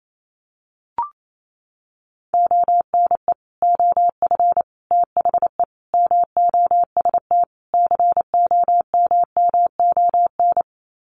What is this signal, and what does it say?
Morse code, heard as a keyed tone, about 28 wpm, 700 Hz. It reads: ONE OF THE MOST COMMON
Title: Sets of 5 Words Encore2 28wpm